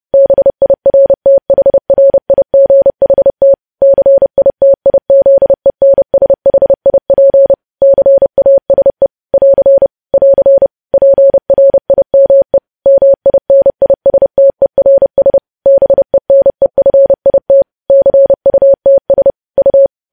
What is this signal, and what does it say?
Morse code, heard as a keyed tone, about 30 wpm, 550 Hz. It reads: BIRTHRIGHT CITIZENSHIP CASE <AR> <AR> PRIME MINISTERS BENEFIT CUTS U
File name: podcast / News headlines-1x-30x30